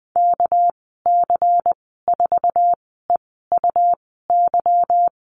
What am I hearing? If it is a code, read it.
Morse code, heard as a keyed tone, about 20 wpm, spaced as for 15 wpm, 700 Hz. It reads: KC4EUY
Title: 200 Lesson 32g - Letter k - Call Signs 20wpm